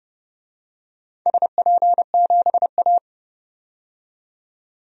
Morse code, heard as a keyed tone, about 30 wpm, 700 Hz. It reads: SP7A
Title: DXCC-100 Call Signs - 4 Characters - 30wpm